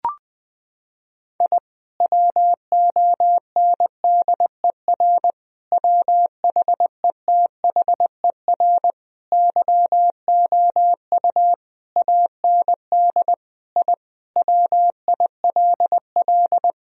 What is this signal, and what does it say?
Morse code, heard as a keyed tone, about 20 wpm, 700 Hz. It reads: I WONDER WHETHER YOU AND I WILL
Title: Sentences from Top 1000 Words - Review Part 3 - 20wpm